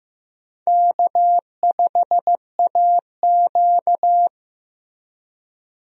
Morse code, heard as a keyed tone, about 15 wpm, 700 Hz. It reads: K5AQ
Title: USA Call Signs - 4 Characters - Repeated 2x - 15wpm